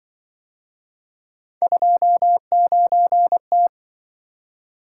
Morse code, heard as a keyed tone, about 25 wpm, 700 Hz. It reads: 29T